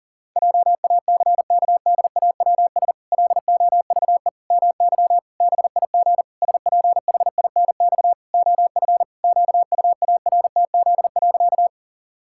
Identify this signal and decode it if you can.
Morse code, heard as a keyed tone, about 40 wpm, 700 Hz. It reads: JACKDAWS LOVE MY BIG SPHINX OF QUARTZ.
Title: Pangram Challenge v2 - 40wpm